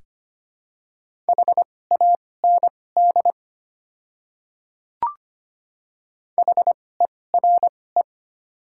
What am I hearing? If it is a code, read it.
Morse code, heard as a keyed tone, about 25 wpm, spaced as for 18 wpm, 700 Hz. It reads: HAND HERE
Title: 2x Character Spacing - Top 300 Words - 25wpm